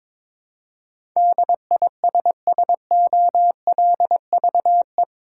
Morse code, heard as a keyed tone, about 22 wpm, 700 Hz. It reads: DISSOLVE
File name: practice / 22wpm / Eight Letter Words - 22wpm